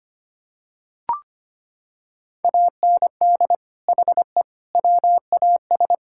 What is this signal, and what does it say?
Morse code, heard as a keyed tone, about 25 wpm, 700 Hz. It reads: AND HE WAS
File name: practice / 25wpm / Sentences with 3-Word Phrases - 25wpm